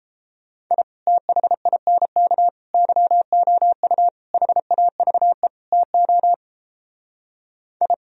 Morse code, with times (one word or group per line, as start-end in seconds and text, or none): 0.7-0.8 I
1.1-2.5 THINK
2.7-4.1 YOU
4.3-5.5 HAVE
5.7-6.3 TO
7.8-7.9 I